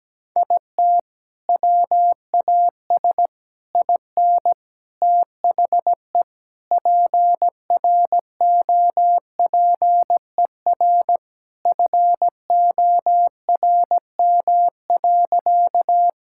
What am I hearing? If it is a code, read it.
Morse code, heard as a keyed tone, about 17 wpm, 700 Hz. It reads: IT WAS IN THE PROPER FORM.
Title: Sentences from Top 1000 Words - Review Part 5 - 17wpm